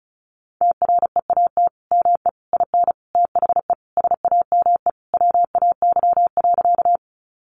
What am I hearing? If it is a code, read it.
Morse code, heard as a keyed tone, about 34 wpm, 700 Hz. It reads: TREAT ME IN THE SAME WAY.
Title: Sentences from Top 1300 Words 35wpm